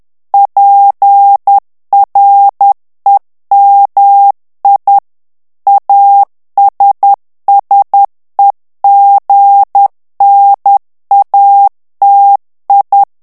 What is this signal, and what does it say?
Morse code, heard as a keyed tone, about 10 wpm, 800 Hz. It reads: PREMI ASSEGNATI
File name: ITALIEN texte clair 2 vitesse 10 mots minute